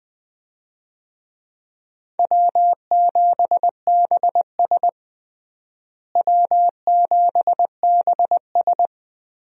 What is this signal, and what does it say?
Morse code, heard as a keyed tone, about 20 wpm, 700 Hz. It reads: W7BS W7BS
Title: USA Call Signs - 4 Characters - Repeated 2x - 20wpm